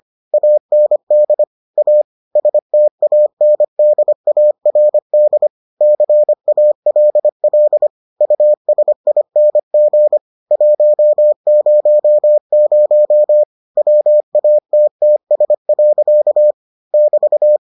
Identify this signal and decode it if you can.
Morse code, heard as a keyed tone, about 25 wpm, 600 Hz. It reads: AND A STANDARD CALL USING 100 WATTS. <BT>